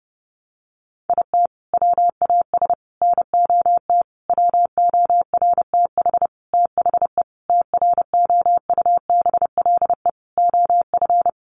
Morse code, heard as a keyed tone, about 30 wpm, 700 Hz. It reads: IT WAS NOT WORTH THE TROUBLE OF
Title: Sentences from Top 1200 Words 30wpm